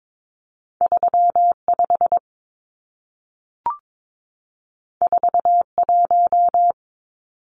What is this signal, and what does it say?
Morse code, heard as a keyed tone, about 22 wpm, 700 Hz. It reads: 35 41
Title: Numbers - 2-Digit 22wpm